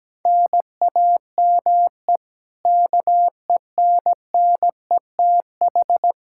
Morse code, heard as a keyed tone, about 17 wpm, 700 Hz. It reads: NAME KENNETH
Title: Practice QSO - Exchange 1 - Beginning - 17wpm